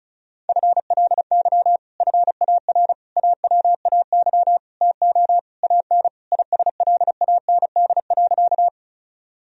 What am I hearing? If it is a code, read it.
Morse code, heard as a keyed tone, about 34 wpm, 700 Hz. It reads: FLY FAR AWAY TO AN ISLAND.
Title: Sentences from Top 1000 Words - Review Part 2 - 35wpm